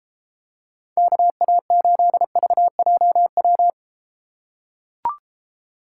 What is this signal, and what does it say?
Morse code, heard as a keyed tone, about 32 wpm, 700 Hz. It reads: KA8VJW